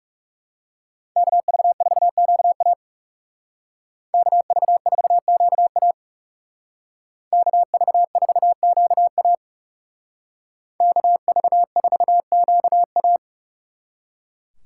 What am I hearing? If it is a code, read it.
Morse code, heard as a keyed tone, about 35 wpm, 700 Hz. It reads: KV4QA KV4QA KV4QA KV4QA
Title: Speed Racing - USA 2x2 Call Signs 30wpm